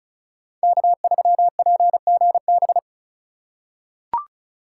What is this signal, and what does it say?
Morse code, heard as a keyed tone, about 34 wpm, 690 Hz. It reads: K3PGB
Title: USA Call Signs - 5 Characters - 2x Character Spacing - 35wpm